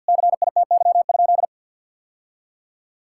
Morse code, heard as a keyed tone, about 50 wpm, 700 Hz. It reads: CITY?
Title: Sentences from Top 1000 Words - Review Part 5 - 50wpm